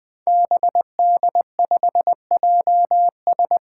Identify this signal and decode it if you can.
Morse code, heard as a keyed tone, about 20 wpm, 700 Hz. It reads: BD5JS
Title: Five Challenging Letter-Number Combinations 20wpm